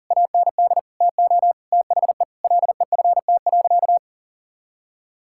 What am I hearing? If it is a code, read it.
Morse code, heard as a keyed tone, about 37 wpm, 700 Hz. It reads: AND TO THE LEFT.
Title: Sentences from Top 1600 Words 40wpm